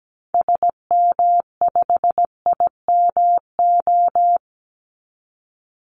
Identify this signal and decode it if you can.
Morse code, heard as a keyed tone, about 17 wpm, 700 Hz. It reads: SM5IMO
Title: Call Signs Repeated 3x - 17wpm